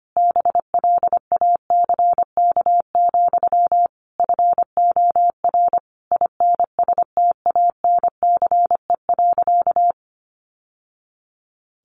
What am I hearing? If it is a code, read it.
Morse code, heard as a keyed tone, about 25 wpm, 700 Hz. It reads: BLACK, FOR INSTANCE.
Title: Sentences from Top 800 Words 25wpm